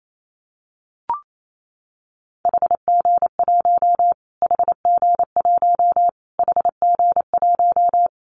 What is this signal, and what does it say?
Morse code, heard as a keyed tone, about 28 wpm, 700 Hz. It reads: HG1 HG1 HG1
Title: Callsign Prefixes for Top 25 DXCC - Repeated 3x - 28wpm